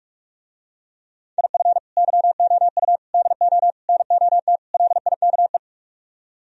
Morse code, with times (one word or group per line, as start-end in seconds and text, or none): 1.4-1.8 IF
2.0-3.0 YOU
3.1-3.7 DO
3.9-4.6 NOT
4.7-5.6 LIKE